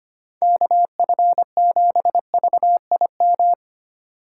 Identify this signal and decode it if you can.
Morse code, heard as a keyed tone, about 25 wpm, 700 Hz. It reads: KF7VIM
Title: USA Call Signs - 6 Characters - 4x Character Spacing - 25wpm